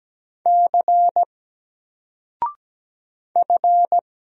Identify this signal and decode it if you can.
Morse code, heard as a keyed tone, about 17 wpm, 700 Hz. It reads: C F
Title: Single Letter-Number Alt 17wpm